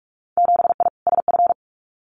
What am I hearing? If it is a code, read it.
Morse code, heard as a keyed tone, about 44 wpm, 700 Hz. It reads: 7I SF
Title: Winter Field Day Exchanges Encore 45wpm